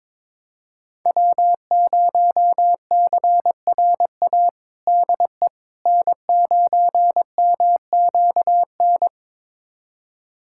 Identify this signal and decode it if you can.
Morse code, heard as a keyed tone, about 22 wpm, 700 Hz. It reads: W0CRA DE N9MQN
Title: USA Callsign DE Callsign - 5 Characters - 22wpm